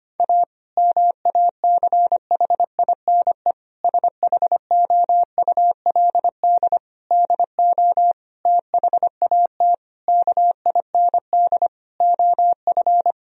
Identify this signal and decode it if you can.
Morse code, heard as a keyed tone, about 25 wpm, 700 Hz. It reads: A MACHINE SHOULD DO THAT KIND OF
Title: Sentences from Top 1000 Words - Review Part 2 - 25wpm